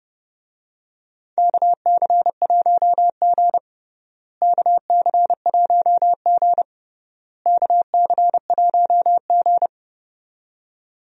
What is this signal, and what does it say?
Morse code, heard as a keyed tone, about 30 wpm, 700 Hz. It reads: KC1G KC1G KC1G
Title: USA Call Signs - 4 Characters - Repeated 3x - 30wpm